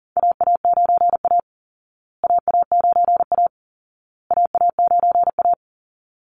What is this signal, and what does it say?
Morse code, heard as a keyed tone, about 37 wpm, 700 Hz. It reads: AA9A AA9A AA9A
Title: Call Signs Repeated 2x - 40wpm